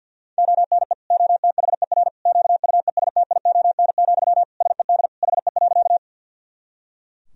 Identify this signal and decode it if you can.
Morse code, heard as a keyed tone, about 51 wpm, 700 Hz. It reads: ONE OTHER QUESTION, SED HE.
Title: Sentences from Top 400 Words and Abbreviations 50wpm